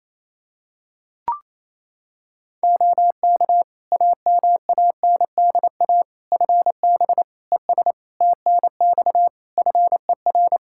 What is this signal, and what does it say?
Morse code, heard as a keyed tone, about 28 wpm, 700 Hz. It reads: OK AMANDA FB ES TNX FER
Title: K1USN SST QSO Practice 28wpm